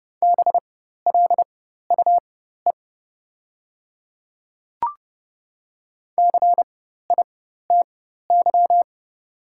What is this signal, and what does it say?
Morse code, heard as a keyed tone, about 30 wpm, spaced as for 14 wpm, 690 Hz. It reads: BLUE CITY
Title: 4x Character Spacing - Top 500 Words - 30wpm